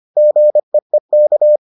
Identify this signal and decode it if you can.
Morse code, heard as a keyed tone, about 25 wpm, 600 Hz. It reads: GEEK